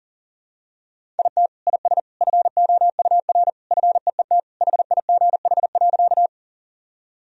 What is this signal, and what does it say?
Morse code, heard as a keyed tone, about 37 wpm, 700 Hz. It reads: IT IS FOUR FEET HIGH.
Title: Sentences from Top 1000 Words - Review Part 1 - 40wpm